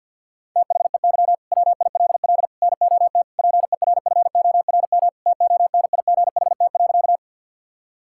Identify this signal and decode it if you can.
Morse code, heard as a keyed tone, about 51 wpm, 700 Hz. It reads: THEY WILL NOT PERFORM TONIGHT.
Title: Sentences from Top 1000 Words - Review Part 5 - 50wpm